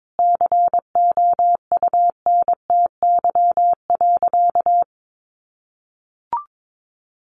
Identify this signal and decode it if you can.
Morse code, heard as a keyed tone, about 22 wpm, 700 Hz. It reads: COUNTY.
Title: Sentences from Top 1200 Words 22wpm